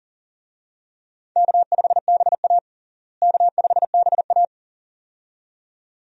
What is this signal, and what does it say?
Morse code, heard as a keyed tone, about 40 wpm, 700 Hz. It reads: K5BA K5BA
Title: USA Call Signs - 4 Characters - Repeated 2x - 40wpm